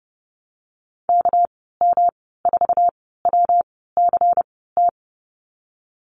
Morse code, heard as a keyed tone, about 31 wpm, spaced as for 17 wpm, 700 Hz. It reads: KM4WCT
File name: Lesson 37g - Letter x - Call Signs 30wpm